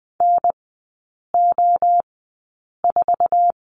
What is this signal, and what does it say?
Morse code, heard as a keyed tone, about 20 wpm, spaced as for 8 wpm, 700 Hz. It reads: NO4
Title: 047 Lesson 9e - Number 4 - Three Characters 20wpm